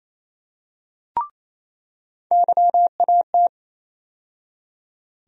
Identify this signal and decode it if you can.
Morse code, heard as a keyed tone, about 28 wpm, 700 Hz. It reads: YAT
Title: USA Call Sign Postfixes - 3 Letters 28wpm